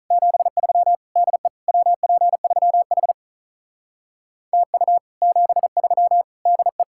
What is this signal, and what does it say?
Morse code, heard as a keyed tone, about 45 wpm, 700 Hz. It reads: 73 DE WP3H TU 73 DE